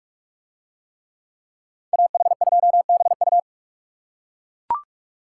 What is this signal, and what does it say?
Morse code, heard as a keyed tone, about 40 wpm, 700 Hz. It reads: AH2BU